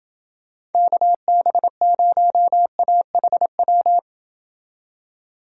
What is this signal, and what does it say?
Morse code, heard as a keyed tone, about 28 wpm, 700 Hz. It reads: KB0AHW